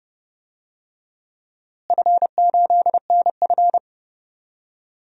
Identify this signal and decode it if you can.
Morse code, heard as a keyed tone, about 30 wpm, 700 Hz. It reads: F8NF
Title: DXCC-100 Call Signs - 4 Characters - Repeated 3x - 30wpm